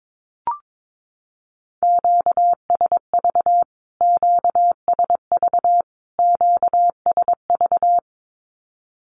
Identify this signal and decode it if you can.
Morse code, heard as a keyed tone, about 22 wpm, 700 Hz. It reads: QSV QSV QSV
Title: Q-Codes Repeated 3x 22wpm